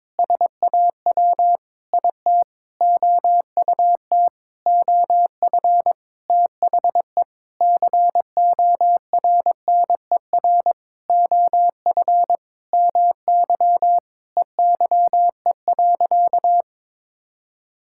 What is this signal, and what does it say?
Morse code, heard as a keyed tone, about 22 wpm, 700 Hz. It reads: SAW IT OUT OF THE CORNER OF MY EYE.
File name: Sentences with 7-Word Phrases - 22wpm